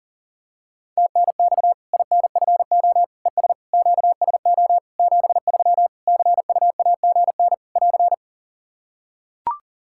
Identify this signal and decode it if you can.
Morse code, heard as a keyed tone, about 37 wpm, 700 Hz. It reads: TNX INFO ES QSO 73 CUAGN <AR>
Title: K1USN SST QSO Practice 40wpm